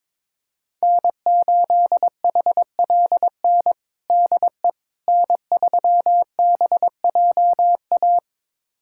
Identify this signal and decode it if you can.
Morse code, heard as a keyed tone, about 22 wpm, 700 Hz. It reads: N8HLN DE N3BJA
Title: USA Callsign DE Callsign - 5 Characters - 22wpm